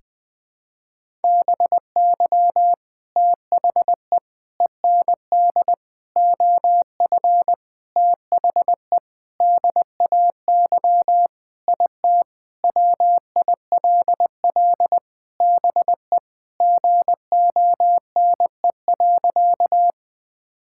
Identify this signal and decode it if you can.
Morse code, heard as a keyed tone, about 20 wpm, 700 Hz. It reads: BY THE END OF THE DAY IT WILL BE GONE.
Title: Sentences with 4-Word Phrases - Spoken Beginning - 20wpm